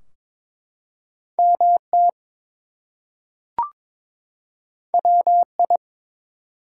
Morse code, heard as a keyed tone, about 22 wpm, 700 Hz. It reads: MT WI
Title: US States and Canadian Provinces 22wpm